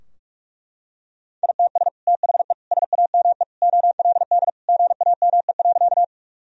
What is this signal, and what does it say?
Morse code, heard as a keyed tone, about 41 wpm, 700 Hz. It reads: ITS THE SAME OLD GAME.
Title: Sentences from Top 1000 Words - Review Part 3 - 45wpm